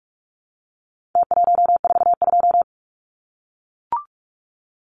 Morse code, heard as a keyed tone, about 44 wpm, 700 Hz. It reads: T142
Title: Numbers - Contest Style - 4-Digit 45wpm